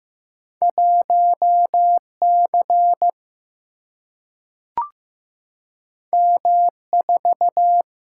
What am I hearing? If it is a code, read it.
Morse code, heard as a keyed tone, about 15 wpm, 700 Hz. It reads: 1 C M 4
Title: Single-Letter Single-Number 15wpm